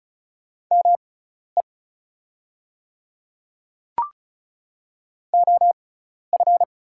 Morse code, heard as a keyed tone, about 36 wpm, spaced as for 12 wpm, 690 Hz. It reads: ME OF
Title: 6x Character Spacing - Top 300 Words - 35wpm